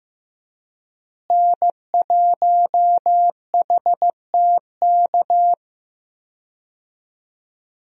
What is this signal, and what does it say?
Morse code, heard as a keyed tone, about 15 wpm, 700 Hz. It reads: N1HTK
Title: USA Call Signs - 5 Characters - Repeated 3x - 15wpm